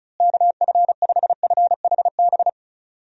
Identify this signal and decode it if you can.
Morse code, heard as a keyed tone, about 34 wpm, 700 Hz. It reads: KF5FHB
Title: USA Call Signs - 6 Characters - Repeated 3x - 35wpm